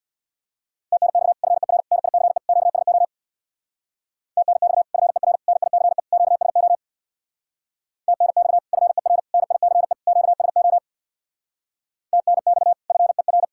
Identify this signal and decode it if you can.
Morse code, heard as a keyed tone, about 65 wpm, 700 Hz. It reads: TNX FER NICE QSO TNX FER NICE QSO TNX FER NICE QSO TNX FER